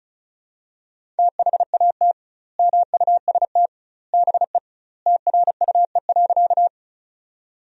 Morse code, heard as a keyed tone, about 36 wpm, 700 Hz. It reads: THAT MUST BE TRUE.
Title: 2x Word-Spacing - Sentences from Top 400 Words - 35wpm